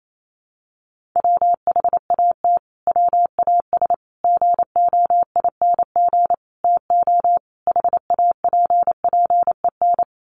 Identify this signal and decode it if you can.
Morse code, heard as a keyed tone, about 28 wpm, 700 Hz. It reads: WHAT WAS GOING TO HAPPEN
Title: Sets of 5 Words Encore 28wpm